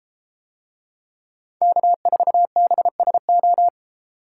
Morse code, heard as a keyed tone, about 31 wpm, 700 Hz. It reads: K4BSO